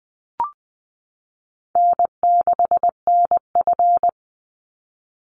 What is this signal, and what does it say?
Morse code, heard as a keyed tone, about 20 wpm, 700 Hz. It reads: N6NF
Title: Easy Call Signs Repeated 3x - 20wpm